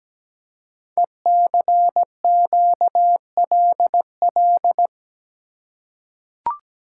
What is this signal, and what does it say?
Morse code, heard as a keyed tone, about 17 wpm, 700 Hz. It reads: ECQLL
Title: Code Groups - 5 Letters - 17wpm